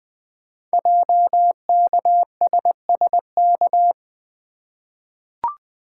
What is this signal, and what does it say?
Morse code, heard as a keyed tone, about 20 wpm, 700 Hz. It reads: JKSSK
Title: Code Groups - 5 Letter:Numbers - 20wpm